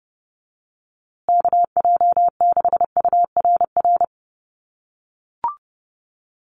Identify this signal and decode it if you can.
Morse code, heard as a keyed tone, about 30 wpm, 700 Hz. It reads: KJ6URR
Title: USA Call Signs Repeated 2x 30wpm